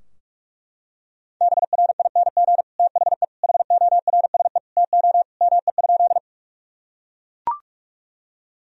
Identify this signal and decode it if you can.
Morse code, heard as a keyed tone, about 43 wpm, 700 Hz. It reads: BRING THE HORSE TO ME?
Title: Sentences from Top 1000 Words - Review Part 3 - 45wpm